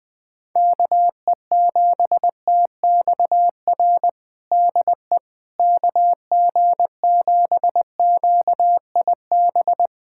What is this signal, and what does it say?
Morse code, heard as a keyed tone, about 20 wpm, 700 Hz. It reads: KE7TXR DE KG7QIB
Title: USA Callsign DE Callsign - 6 Characters - 20wpm